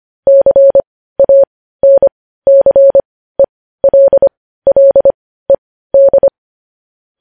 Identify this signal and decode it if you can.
Morse code, heard as a keyed tone, about 25 wpm, spaced as for 15 wpm, 550 Hz. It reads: CANCELLED
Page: News headlines in morse code at 15 wpm